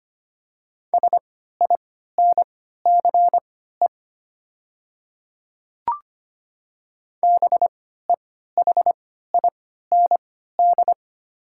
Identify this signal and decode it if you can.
Morse code, heard as a keyed tone, about 25 wpm, spaced as for 14 wpm, 700 Hz. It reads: SINCE BEHIND
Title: 3x Character Spacing - Top 500 Words - 25wpm